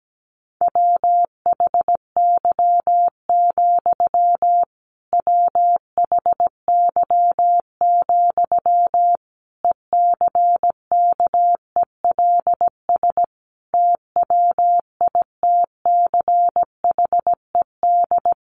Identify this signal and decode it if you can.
Morse code, heard as a keyed tone, about 17 wpm, 700 Hz. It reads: WHY, WHY, ECKELS TWITCHED
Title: A Sound Of Thunder 17wpm